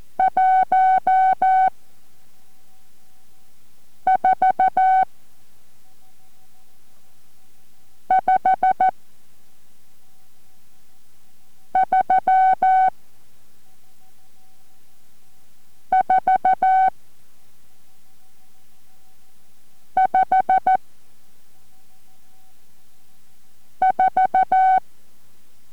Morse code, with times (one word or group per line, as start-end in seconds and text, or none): 0.2-1.7 1
4.1-5.0 4
8.1-8.9 5
11.8-12.9 3
15.9-16.9 4
20.0-20.8 5
23.8-24.8 4